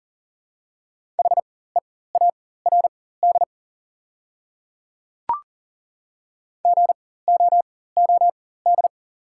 Morse code, heard as a keyed tone, about 40 wpm, spaced as for 19 wpm, 700 Hz. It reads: HEARD GOOD
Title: 4x Character Spacing - Top 500 Words - 40wpm